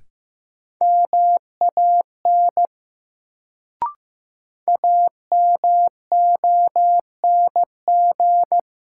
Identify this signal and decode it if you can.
Morse code, heard as a keyed tone, about 15 wpm, 700 Hz. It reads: MAN AMONG
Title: Sets of 2 Words - New Word Drill - 15wpm